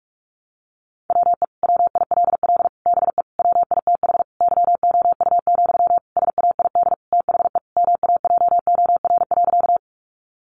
Morse code, heard as a keyed tone, about 44 wpm, 700 Hz. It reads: WE WILL BE WITH YOU, SAID THE MAJOR.
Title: Sentences from Top 600 Words 45wpm